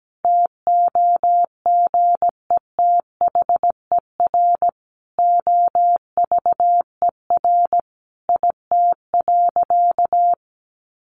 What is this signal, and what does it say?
Morse code, heard as a keyed tone, about 17 wpm, 700 Hz. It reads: TOGETHER OVER IT.
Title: Sentences from Top 1500 Words 17wpm